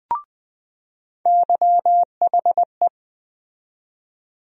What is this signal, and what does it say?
Morse code, heard as a keyed tone, about 20 wpm, 700 Hz. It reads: YHE
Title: USA Call Sign Postfixes - 3 Letters 20wpm